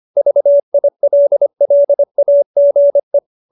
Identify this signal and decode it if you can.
Morse code, heard as a keyed tone, about 25 wpm, 590 Hz. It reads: VILLAGE